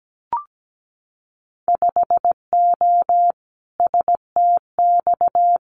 Morse code, heard as a keyed tone, about 17 wpm, 690 Hz. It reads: 5O STX
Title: Winter Field Day Exchanges 17wpm